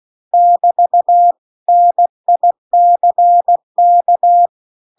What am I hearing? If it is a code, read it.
Morse code, heard as a keyed tone, about 16 wpm, 700 Hz. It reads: <BT> NICK
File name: CW-The-Big-Twoheaded-River-2-16WPM